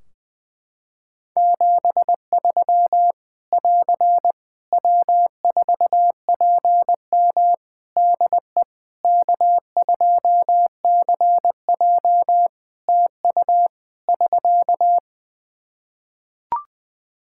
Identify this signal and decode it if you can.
Morse code, heard as a keyed tone, about 19 wpm, 700 Hz. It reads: 73 <AR> W4PM DE K2CJ TU <SK>
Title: K1USN SST QSO Practice 20wpm